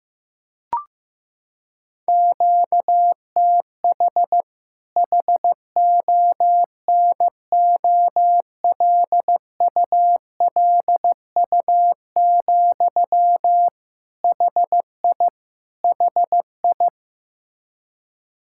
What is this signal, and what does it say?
Morse code, heard as a keyed tone, about 15 wpm, 700 Hz. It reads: QTH HONOLULU, HI HI
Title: Practice QSO - Exchange 1 - Beginning - 15wpm